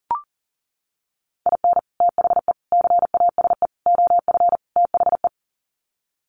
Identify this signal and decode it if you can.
Morse code, heard as a keyed tone, about 40 wpm, 700 Hz. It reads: IN THE CASE OF THE
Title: Sets of 5 Words 40wpm